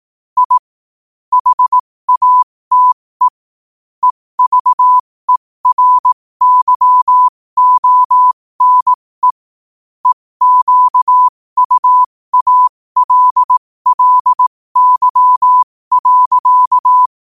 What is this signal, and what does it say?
Morse code, heard as a keyed tone, about 18 wpm, spaced as for 15 wpm, 1000 Hz. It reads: I HATE EVERYONE EQUALLY.
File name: qotdfiles / 15wpm